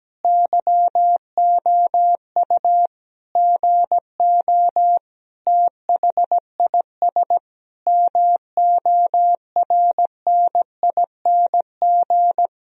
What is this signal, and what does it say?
Morse code, heard as a keyed tone, about 17 wpm, 700 Hz. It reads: YOU GO THIS MORNING
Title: Sentences from Top 1000 Words - Review Part 4 - 17wpm